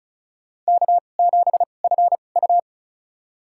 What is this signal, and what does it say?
Morse code, heard as a keyed tone, about 34 wpm, spaced as for 25 wpm, 700 Hz. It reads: K7FU
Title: USA Call Signs - 4 Characters - 2x Character Spacing - 35wpm